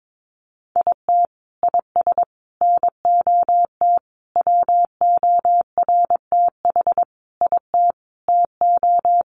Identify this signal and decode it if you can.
Morse code, heard as a keyed tone, about 22 wpm, 700 Hz. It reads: IT IS NOT WORTH IT TO
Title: Sentences from Top 800 Words 22wpm